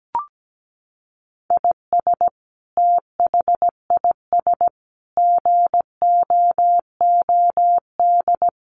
Morse code, heard as a keyed tone, about 17 wpm, 700 Hz. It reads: IS THIS GOOD
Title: Sentences from Top 700 Words 17wpm